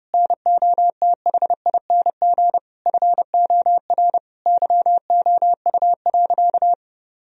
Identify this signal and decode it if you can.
Morse code, heard as a keyed tone, about 32 wpm, 700 Hz. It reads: NOTHING FOR YOU.
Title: Sentences from Top 1000 Words - Review Part 3 - 30wpm